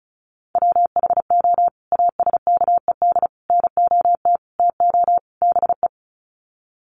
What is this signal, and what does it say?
Morse code, heard as a keyed tone, about 34 wpm, 700 Hz. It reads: WHO ASKED NOT TO BE
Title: Sets of 5 Words Encore2 35wpm